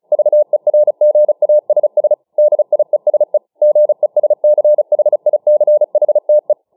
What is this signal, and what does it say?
Morse code, heard as a keyed tone, about 36 wpm, 600 Hz. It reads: VERGASS DIESE GESCHICHTE